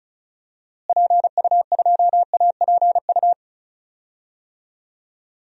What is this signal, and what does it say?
Morse code, heard as a keyed tone, about 36 wpm, 700 Hz. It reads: PU2APU
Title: DXCC-100 Call Signs - 6 Characters - Repeated 2x - 35wpm